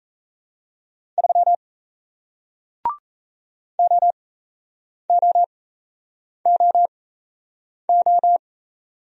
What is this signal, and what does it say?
Morse code, heard as a keyed tone, about 38 wpm, spaced as for 8 wpm, 700 Hz. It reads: 3 OOOO